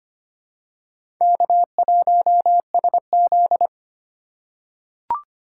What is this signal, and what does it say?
Morse code, heard as a keyed tone, about 25 wpm, 700 Hz. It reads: K1SZ
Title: USA Call Signs - 4 Characters - 4x Character Spacing - 25wpm